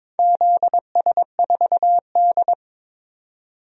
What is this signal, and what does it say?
Morse code, heard as a keyed tone, about 22 wpm, 700 Hz. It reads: ZS4D
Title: DXCC-100 Call Signs - 4 Characters - 22wpm